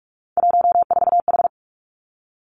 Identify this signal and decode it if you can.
Morse code, heard as a keyed tone, about 46 wpm, 700 Hz. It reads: 14H
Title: Three Challenging Letter-Number Combinations 45wpm